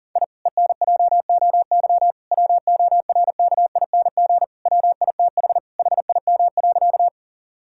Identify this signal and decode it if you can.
Morse code, heard as a keyed tone, about 40 wpm, 700 Hz. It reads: I ENJOY WORKING WITH HIM.
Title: Sentences from Top 1000 Words - Review Part 5 - 40wpm